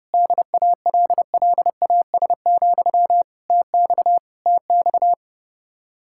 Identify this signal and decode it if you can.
Morse code, heard as a keyed tone, about 31 wpm, 700 Hz. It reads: DALLAS, TX TX